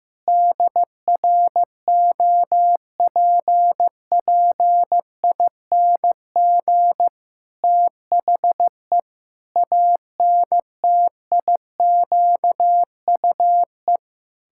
Morse code, heard as a keyed tone, about 15 wpm, 700 Hz. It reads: DROPPING THE ANTIQUE
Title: Pangram Challenge v2 - 15wpm